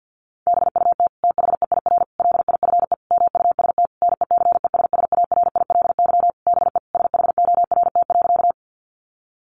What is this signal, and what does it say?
Morse code, heard as a keyed tone, about 50 wpm, 700 Hz. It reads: BUT THEIR LIFE MUST NECESSARILY BE SHORT.
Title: Sentences from Top 1400 Words 50wpm